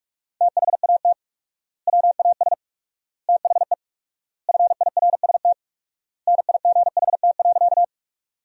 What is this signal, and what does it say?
Morse code, heard as a keyed tone, about 46 wpm, 700 Hz. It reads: THAT WAS THE FIRST NIGHT.
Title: 4x Word-Spacing - Sentences from Top 300 Words - 45wpm